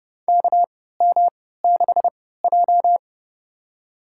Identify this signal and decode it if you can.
Morse code, heard as a keyed tone, about 30 wpm, spaced as for 17 wpm, 700 Hz. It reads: KM6J
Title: USA Call Signs - 3x Character Spacing - 30wpm